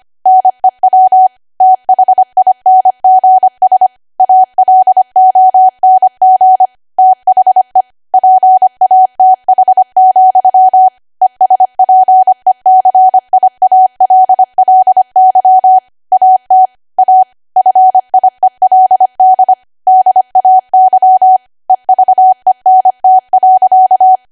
Morse code, heard as a keyed tone, about 25 wpm, 750 Hz. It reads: NEW THINGS ALONG THE PATH, ESPECIALLY AT A FIELD DAY EVENT.
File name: Archive / 25 WPM / 25WPM